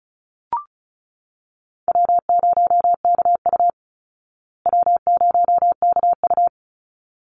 Morse code, heard as a keyed tone, about 34 wpm, 700 Hz. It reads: W0KU W0KU
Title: Easy Call Signs Repeated 2x - 35wpm